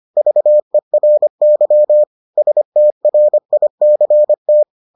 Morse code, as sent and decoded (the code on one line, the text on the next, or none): ...- . .-. -.-- / ... - .-. .. -.-. -
VERY STRICT